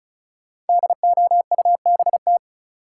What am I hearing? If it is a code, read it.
Morse code, heard as a keyed tone, about 34 wpm, 700 Hz. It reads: DOUBT
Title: Sentences from Top 1500 Words - New Word Drill Alt - 35wpm